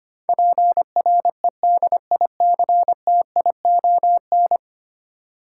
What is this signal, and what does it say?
Morse code, heard as a keyed tone, about 25 wpm, 700 Hz. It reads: PREDICTION
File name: 10 Letter Words - First 3 Letters Given - 25wpm